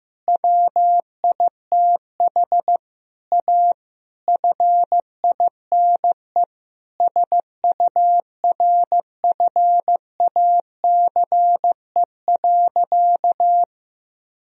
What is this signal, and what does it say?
Morse code, heard as a keyed tone, about 15 wpm, 700 Hz. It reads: WITH A FINE SURFACE.
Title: Sentences from Top 1000 Words - Review Part 3 - 15wpm